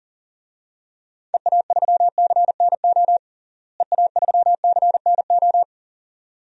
Morse code, heard as a keyed tone, about 40 wpm, 700 Hz. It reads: EA3CNO EA3CNO
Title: DXCC-100 Call Signs - 6 Characters - Repeated 2x - 40wpm